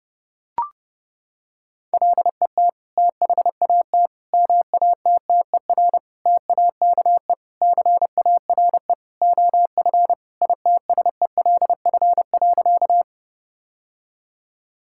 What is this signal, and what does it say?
Morse code, heard as a keyed tone, about 30 wpm, 700 Hz. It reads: LET THAT MATTER TAKE CARE OF ITSELF.
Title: Sentences from Top 1000 Words - Review Part 5 - 30wpm